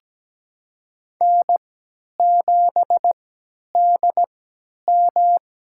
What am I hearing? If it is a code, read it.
Morse code, heard as a keyed tone, about 17 wpm, spaced as for 10 wpm, 700 Hz. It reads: N7DM
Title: USA Call Signs - 4 Characters - 3x Character Spacing - 17wpm